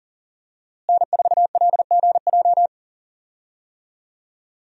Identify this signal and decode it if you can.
Morse code, heard as a keyed tone, about 40 wpm, 700 Hz. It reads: N4LGJ
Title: USA Call Signs - 5 Characters - Repeated 3x - 40wpm